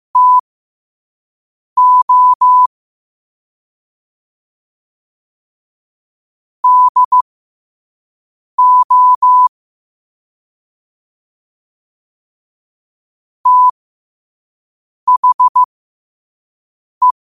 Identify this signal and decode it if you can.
Morse code, heard as a keyed tone, about 15 wpm, spaced as for 5 wpm, 1000 Hz. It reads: TO DO THE
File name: qotdfiles / 5wpm